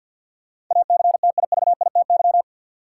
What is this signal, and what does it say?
Morse code, heard as a keyed tone, about 51 wpm, 700 Hz. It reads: ACTIVITY
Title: Sentences from Top 1000 Words - Review Part 4 - New Word Drill - 50wpm